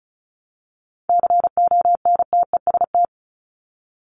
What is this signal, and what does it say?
Morse code, heard as a keyed tone, about 34 wpm, 700 Hz. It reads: CONTEST
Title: Top 1501-1750 Words 35wpm